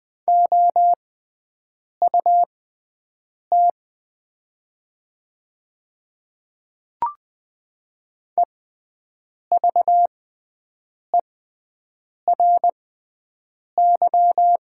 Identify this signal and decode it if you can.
Morse code, heard as a keyed tone, about 20 wpm, spaced as for 7 wpm, 700 Hz. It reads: OUT EVERY
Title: 6x Character Spacing - Top 200 Words - 20wpm